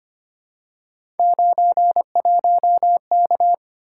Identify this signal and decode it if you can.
Morse code, heard as a keyed tone, about 25 wpm, 700 Hz. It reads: 91K
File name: Three Letter-Number Combinations - 25wpm